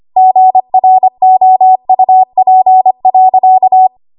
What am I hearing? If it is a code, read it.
Morse code, heard as a keyed tone, about 25 wpm, 750 Hz. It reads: GROUP.
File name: Morse / 25WPM